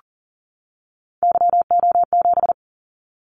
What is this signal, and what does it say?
Morse code, heard as a keyed tone, about 40 wpm, 700 Hz. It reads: YO7
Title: Callsign Prefixes for Top 25 DXCC - 40wpm